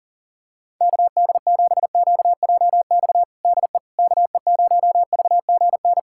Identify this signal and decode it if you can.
Morse code, heard as a keyed tone, about 40 wpm, 700 Hz. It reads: KD7QJX DE KE0VGN
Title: USA Callsign DE Callsign 40wpm